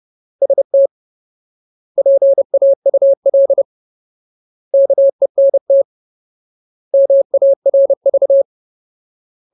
Morse code, heard as a keyed tone, about 30 wpm, 550 Hz. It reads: ST PAUL KENT MARV
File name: Ultimate Word List / 30WPM Ultimate